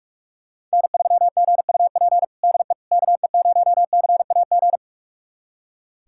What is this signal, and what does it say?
Morse code, heard as a keyed tone, about 46 wpm, 700 Hz. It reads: N3GUP DE KE0CAG
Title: USA Callsign DE Callsign 45wpm